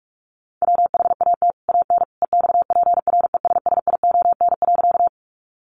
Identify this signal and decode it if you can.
Morse code, heard as a keyed tone, about 44 wpm, 700 Hz. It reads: WHAT AN EXPRESSION.
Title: Sentences from Top 1200 Words 45wpm